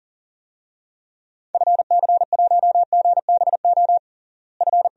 Morse code, heard as a keyed tone, about 40 wpm, 700 Hz. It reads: FC1GBO F
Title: DXCC-100 Call Signs - 6 Characters - Repeated 2x - 40wpm